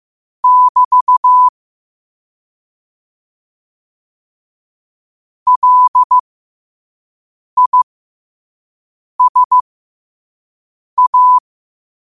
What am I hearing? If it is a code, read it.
Morse code, heard as a keyed tone, about 15 wpm, spaced as for 5 wpm, 1000 Hz. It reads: <BT> LISA
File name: qotdfiles / 5wpm